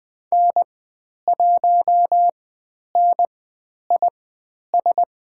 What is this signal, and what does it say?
Morse code, heard as a keyed tone, about 20 wpm, spaced as for 10 wpm, 700 Hz. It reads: N1NIS
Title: 095 Lesson 16g - Letter c - Call Signs 20wpm